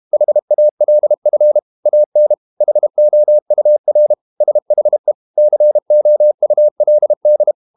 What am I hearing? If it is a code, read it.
Morse code, heard as a keyed tone, about 32 wpm, 600 Hz. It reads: HALF AN HOUR SHE COULD
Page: Never 31wpm | CW med Gnister